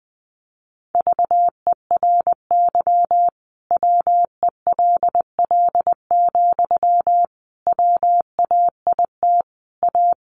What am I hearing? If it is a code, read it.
Morse code, heard as a keyed tone, about 20 wpm, 700 Hz. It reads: VERY WELL, WAIT A
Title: Sentences from Top 500 Words 20wpm